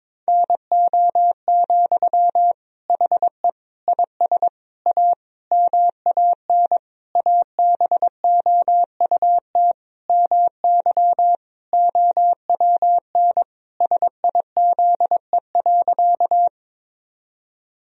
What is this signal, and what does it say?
Morse code, heard as a keyed tone, about 22 wpm, 700 Hz. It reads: NO, HE IS A MAN ABOUT MY OWN SIZE.
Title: Sentences from Top 1000 Words - Review Part 2 - 22wpm